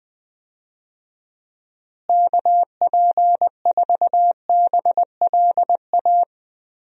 Morse code, heard as a keyed tone, about 21 wpm, 700 Hz. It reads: KP4BLA
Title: DXCC-100 Call Signs - 6 Characters - Repeated 3x - 20wpm